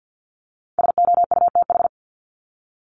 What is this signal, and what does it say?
Morse code, heard as a keyed tone, about 50 wpm, 700 Hz. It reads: SOUTH
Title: Top 500 Words 50wpm